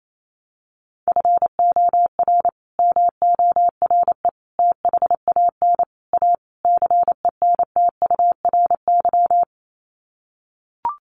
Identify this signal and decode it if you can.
Morse code, heard as a keyed tone, about 28 wpm, 700 Hz. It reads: FOR MORE THAN A CENTURY
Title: Sets of 5 Words Encore2 28wpm